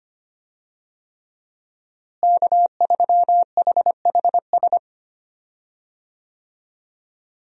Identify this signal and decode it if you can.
Morse code, heard as a keyed tone, about 25 wpm, 700 Hz. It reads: K3HHS
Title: USA Call Signs - 5 Characters - Repeated 3x - 25wpm